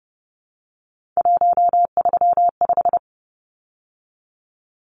Morse code, heard as a keyed tone, about 30 wpm, 700 Hz. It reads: 135
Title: Numbers - 3-Digit 30wpm